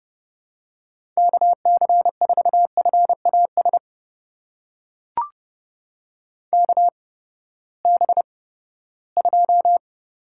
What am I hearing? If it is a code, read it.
Morse code, heard as a keyed tone, about 30 wpm, spaced as for 17 wpm, 700 Hz. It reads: KC4FAS KB2
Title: USA Call Signs - 6 Characters - 8x Character Spacing - 30wpm